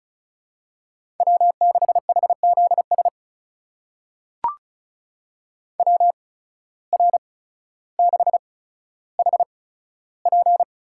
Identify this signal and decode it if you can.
Morse code, heard as a keyed tone, about 36 wpm, spaced as for 15 wpm, 700 Hz. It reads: W6HZS WR6HP
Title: USA Call Signs - 8x Character Spacing - 35wpm